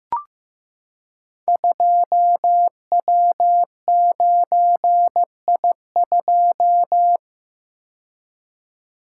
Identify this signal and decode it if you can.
Morse code, heard as a keyed tone, about 15 wpm, 700 Hz. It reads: 2W9I2
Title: Code Groups - 5 Letter:Numbers - 15wpm